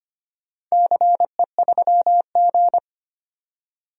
Morse code, heard as a keyed tone, about 25 wpm, 700 Hz. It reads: CE3G
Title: DXCC-100 Call Signs - 4 Characters - 25wpm